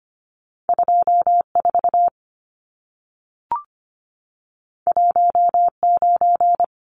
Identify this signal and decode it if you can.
Morse code, heard as a keyed tone, about 25 wpm, 700 Hz. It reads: 24 19
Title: Numbers - 2-Digit 25wpm